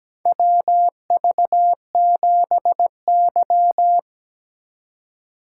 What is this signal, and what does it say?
Morse code, heard as a keyed tone, about 17 wpm, 700 Hz. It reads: WV7Y
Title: USA Call Signs - 4 Characters - 3x Character Spacing - 17wpm